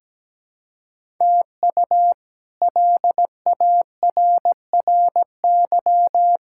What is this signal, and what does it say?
Morse code, heard as a keyed tone, about 17 wpm, 700 Hz. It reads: TU LARRY
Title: K1USN SST QSO Practice - 17wpm